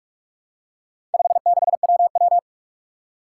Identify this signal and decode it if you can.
Morse code, heard as a keyed tone, about 46 wpm, 700 Hz. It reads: 56WW
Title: Four Challenging Letter-Number Combinations 45wpm